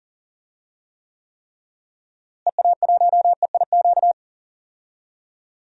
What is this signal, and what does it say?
Morse code, heard as a keyed tone, about 37 wpm, 700 Hz. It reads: EA1EIQ
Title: DXCC-100 Call Signs - 6 Characters - Repeated 3x - 40wpm